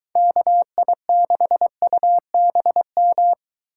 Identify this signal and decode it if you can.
Morse code, heard as a keyed tone, about 23 wpm, 700 Hz. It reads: KI6UBM